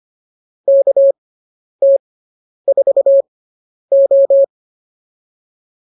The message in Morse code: KT4O

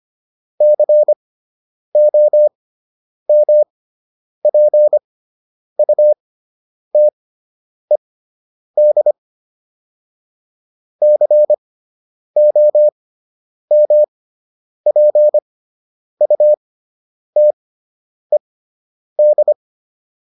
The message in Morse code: COMPUTED COMPUTED

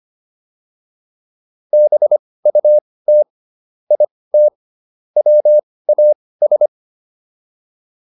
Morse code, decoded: BUT IT WAS

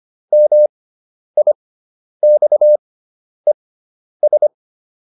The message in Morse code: MIXES

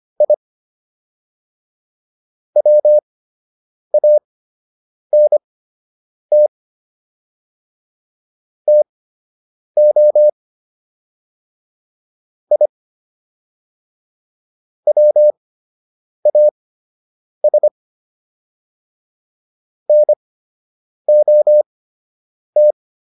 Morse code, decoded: I WANT TO I WAS NOT

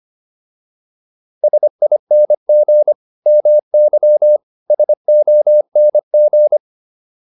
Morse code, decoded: SING MY SONG

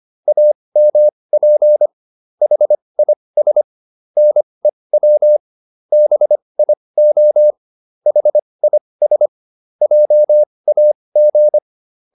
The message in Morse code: AMP HIS NEW BIO HIS JAG

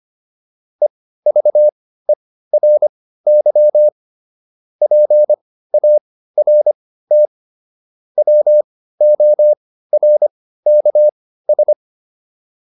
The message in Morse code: EVERY PART WORKS

